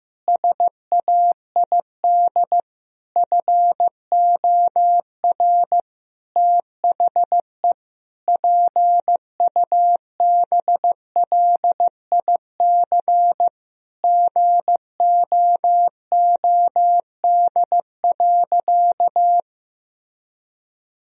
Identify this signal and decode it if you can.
Morse code, heard as a keyed tone, about 15 wpm, 700 Hz. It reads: SAID FOR THE PUBLIC GOOD.
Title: Sentences from Top 1000 Words - Review Part 3 - 15wpm